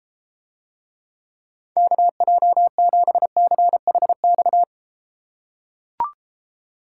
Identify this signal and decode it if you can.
Morse code, heard as a keyed tone, about 33 wpm, 700 Hz. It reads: KJ7CHX